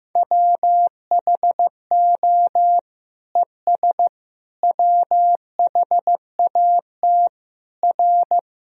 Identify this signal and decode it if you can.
Morse code, heard as a keyed tone, about 15 wpm, 700 Hz. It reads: WHO ES WHAT R
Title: Sentences from Top 100 Words and Abbreviations 15wpm